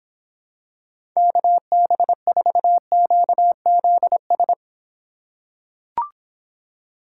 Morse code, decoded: KB4QZS

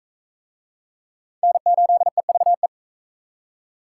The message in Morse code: N8EVE

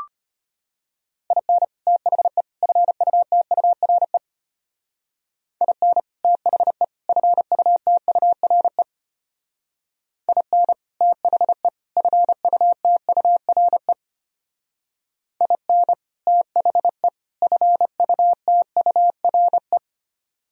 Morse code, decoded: IN THE FUTURE IN THE FUTURE IN THE FUTURE IN THE FUTURE

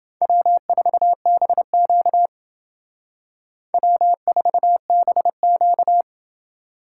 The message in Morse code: W4BQ W4BQ

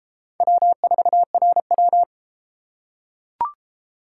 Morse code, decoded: W4RW

